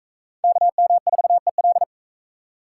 Morse code, heard as a keyed tone, about 37 wpm, 700 Hz. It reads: KM4EL